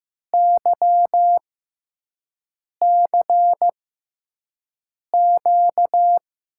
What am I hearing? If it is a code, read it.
Morse code, heard as a keyed tone, about 15 wpm, spaced as for 5 wpm, 700 Hz. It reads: YCQ
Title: USA Call Signs - 6 Characters - 6x Character Spacing - 15wpm